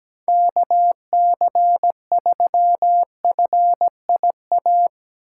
Morse code, heard as a keyed tone, about 17 wpm, 700 Hz. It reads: KC3FIA
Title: USA Call Signs - 6 Characters - 6x Character Spacing - 17wpm